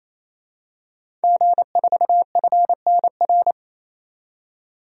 Morse code, heard as a keyed tone, about 28 wpm, 700 Hz. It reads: G4FNR
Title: DXCC-100 Call Signs - 5 Characters - 28wpm